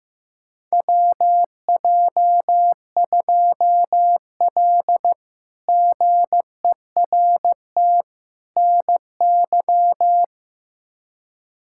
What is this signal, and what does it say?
Morse code, heard as a keyed tone, about 15 wpm, 700 Hz. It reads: WJ2L GERT NY
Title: K1USN SST QSO Practice - 15wpm